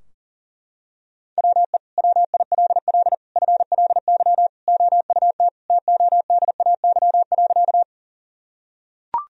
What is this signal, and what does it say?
Morse code, heard as a keyed tone, about 40 wpm, 700 Hz. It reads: WE WILL FLY OUT TODAY.
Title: Sentences from Top 1000 Words - Review Part 2 - 40wpm